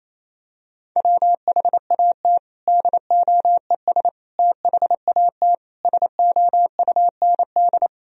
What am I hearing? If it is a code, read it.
Morse code, heard as a keyed tone, about 28 wpm, 700 Hz. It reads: WHAT DOES THAT SOUND
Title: Sentences from Top 1000 Words - Review Part 1 - 28wpm